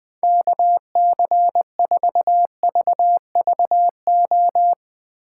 Morse code, heard as a keyed tone, about 20 wpm, 700 Hz. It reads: KC4VVO
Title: Speed Racing - USA 2x3 Call Signs 17wpm